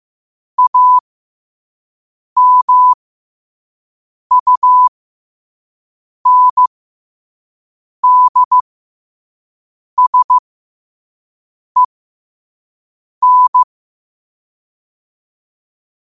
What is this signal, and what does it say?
Morse code, heard as a keyed tone, about 15 wpm, spaced as for 5 wpm, 1000 Hz. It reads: AMUNDSEN
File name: 2024-10-07-5wpm